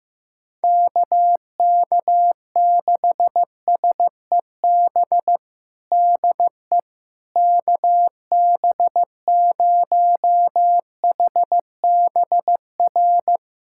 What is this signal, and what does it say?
Morse code, heard as a keyed tone, about 15 wpm, 700 Hz. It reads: KK6SEB DE KB0HBR
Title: USA Callsign DE Callsign - 6 Characters - 15wpm